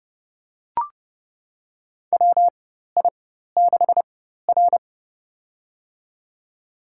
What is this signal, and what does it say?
Morse code, heard as a keyed tone, about 30 wpm, spaced as for 14 wpm, 700 Hz. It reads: WI6R
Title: USA Call Signs - 4 Characters - 4x Character Spacing - 30wpm